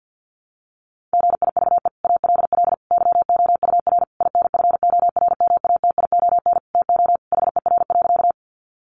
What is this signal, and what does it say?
Morse code, heard as a keyed tone, about 48 wpm, 700 Hz. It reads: GIVE ALL YOUR INFORMATION TO HER.
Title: Sentences from Top 600 Words 50wpm